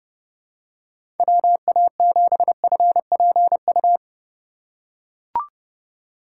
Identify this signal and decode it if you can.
Morse code, heard as a keyed tone, about 30 wpm, 700 Hz. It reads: WA7FPU
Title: USA Call Signs - 6 Characters - 3x Character Spacing - 30wpm